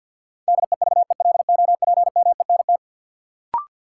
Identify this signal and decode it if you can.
Morse code, heard as a keyed tone, about 52 wpm, 700 Hz. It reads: DEVELOPMENT
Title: Sentences from Top 1400 Words - New Word Drill Alt - 50wpm